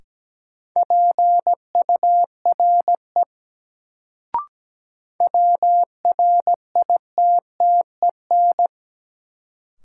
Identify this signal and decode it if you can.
Morse code, heard as a keyed tone, about 17 wpm, 700 Hz. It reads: PURE WRITTEN
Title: Sentences from Top 1500 Words - New Word Drill - 17wpm